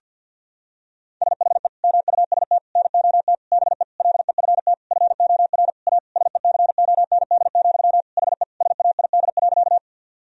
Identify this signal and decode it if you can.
Morse code, heard as a keyed tone, about 52 wpm, 700 Hz. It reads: SHE MUST NOT BE LEFT FOR A SECOND, HE SAID.
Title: Sentences from Top 1000 Words - Review Part 1 - 50wpm